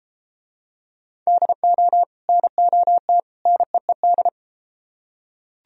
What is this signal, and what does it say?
Morse code, heard as a keyed tone, about 31 wpm, 700 Hz. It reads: DO NOT NEED